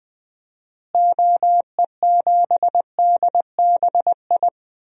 Morse code, heard as a keyed tone, about 20 wpm, 700 Hz. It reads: OE7DBI
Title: DXCC-100 Call Signs - 6 Characters - Repeated 3x - 20wpm